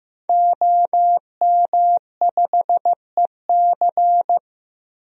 Morse code, heard as a keyed tone, about 15 wpm, 700 Hz. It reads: OM5EC
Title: DXCC-100 Call Signs - 5 Characters - Repeated 3x - 15wpm